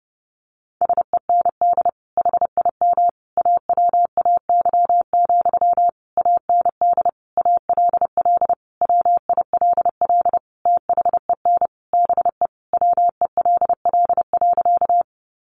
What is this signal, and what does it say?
Morse code, heard as a keyed tone, about 30 wpm, 700 Hz. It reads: SEND HIM AWAY, AND ALL WILL THEN BE WELL.
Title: Sentences from Top 1500 Words 30wpm